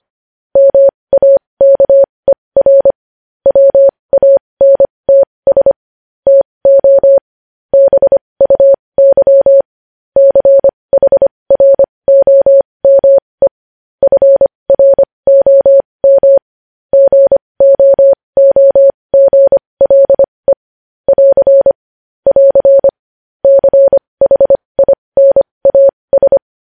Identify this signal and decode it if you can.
Morse code, heard as a keyed tone, about 25 wpm, spaced as for 20 wpm, 550 Hz. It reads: MAKER WANTS TO BUY CHROME FROM GOOGLE <AR> <AR> CHINAS
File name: podcast / News headlines-1x-25x20